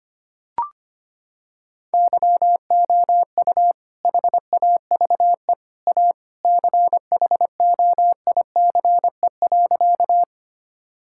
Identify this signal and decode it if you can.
Morse code, heard as a keyed tone, about 25 wpm, 700 Hz. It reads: YOU HAVE A CHOICE.
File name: practice / 25wpm / Sentences with 3-Word Phrases - 25wpm